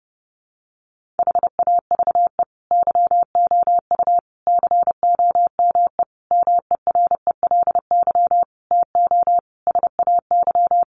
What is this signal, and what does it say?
Morse code, heard as a keyed tone, about 30 wpm, 700 Hz. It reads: HAVE YOU COME MERELY TO SAY
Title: Sentences from Top 1400 Words 30wpm